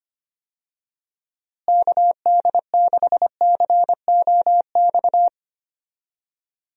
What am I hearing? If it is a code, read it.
Morse code, heard as a keyed tone, about 25 wpm, 700 Hz. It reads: KD6COX
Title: USA Call Signs - 6 Characters - Repeated 2x - 25wpm